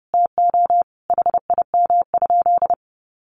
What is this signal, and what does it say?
Morse code, heard as a keyed tone, about 28 wpm, 700 Hz. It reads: TO HIM?
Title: Sentences from Top 1100 Words 30wpm